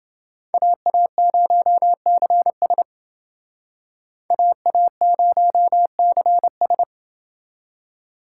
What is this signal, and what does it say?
Morse code, heard as a keyed tone, about 28 wpm, 700 Hz. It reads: AA0CS AA0CS